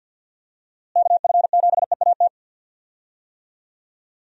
Morse code, heard as a keyed tone, about 50 wpm, 700 Hz. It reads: KF7EAT
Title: USA Call Signs - 6 Characters - Repeated 3x - 50wpm